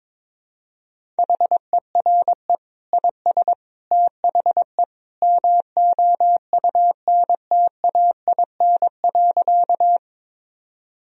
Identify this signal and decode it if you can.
Morse code, heard as a keyed tone, about 22 wpm, 700 Hz. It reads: HERE IS THE MOUNTAIN.
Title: Sentences from Top 1000 Words - Review Part 1 - 22wpm